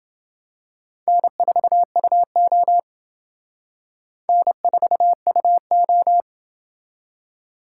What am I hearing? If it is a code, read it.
Morse code, heard as a keyed tone, about 28 wpm, 700 Hz. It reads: N4UO N4UO